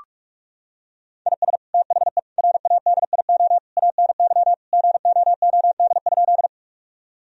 Morse code, heard as a keyed tone, about 42 wpm, 700 Hz. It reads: IS THE RADIO ANY GOOD?
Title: Sentences from Top 1000 Words - Review Part 3 - 45wpm